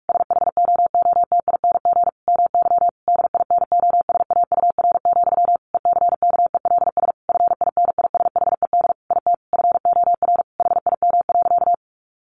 Morse code, heard as a keyed tone, about 44 wpm, 700 Hz. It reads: SHOOTING MY DINOSAUR, ECKELS FINISHED IT FOR HIM.
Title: A Sound Of Thunder 45wpm